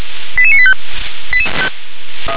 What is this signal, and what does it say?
Five-tone selective call (ZVEI), heard as quick runs of five tones, tones of about 70 ms. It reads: 88985
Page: /357/SDR/ZVEI/88957/